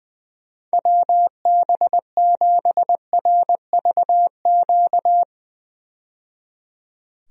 Morse code, heard as a keyed tone, about 20 wpm, 700 Hz. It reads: WB7RVQ
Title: USA Call Signs - 6 Characters - 20wpm